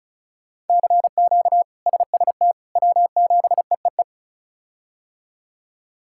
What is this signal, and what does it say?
Morse code, heard as a keyed tone, about 36 wpm, 700 Hz. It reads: CQ SST W7EEE
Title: K1USN SST QSO Practice - 35wpm